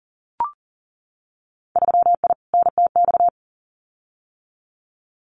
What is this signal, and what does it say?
Morse code, heard as a keyed tone, about 40 wpm, 700 Hz. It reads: 3I NTX
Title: Winter Field Day Exchanges 40wpm